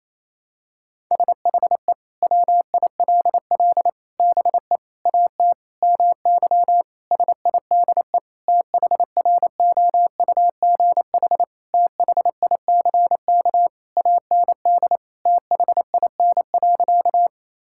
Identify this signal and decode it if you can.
Morse code, heard as a keyed tone, about 28 wpm, 700 Hz. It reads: SHE WILL BE AT MY SIDE THROUGH THICK AND THIN.
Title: Sentences from Top 1600 Words 28wpm